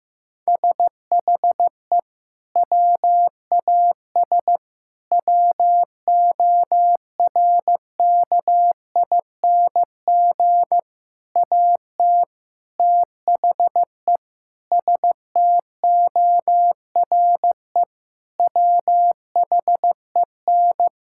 Morse code, heard as a keyed tone, about 15 wpm, 700 Hz. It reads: SHE WAS WORKING AT THE STORE WHEN
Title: Sentences from Top 1000 Words - Review Part 5 - 15wpm